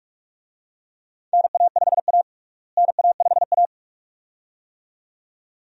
Morse code, heard as a keyed tone, about 43 wpm, 700 Hz. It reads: NA5A NA5A
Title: USA Call Signs - 4 Characters - Repeated 2x - 45wpm